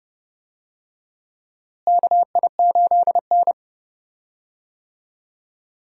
Morse code, heard as a keyed tone, about 31 wpm, 700 Hz. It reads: KI8N
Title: USA Call Signs - 4 Characters - Repeated 3x - 30wpm